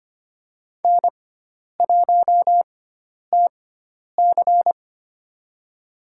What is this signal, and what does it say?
Morse code, heard as a keyed tone, about 25 wpm, spaced as for 10 wpm, 700 Hz. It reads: N1TC
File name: Lesson 16g - Letter c - Call Signs 25wpm